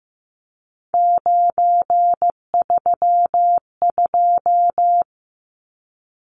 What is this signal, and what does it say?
Morse code, heard as a keyed tone, about 15 wpm, 700 Hz. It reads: 932
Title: Numbers - 3-Digit 15wpm